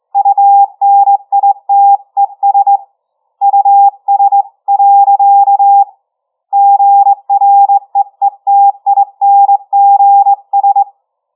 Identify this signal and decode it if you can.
Morse code, heard as a keyed tone, about 22 wpm, 800 Hz. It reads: UNITES US. GREETINGS
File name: challenge-133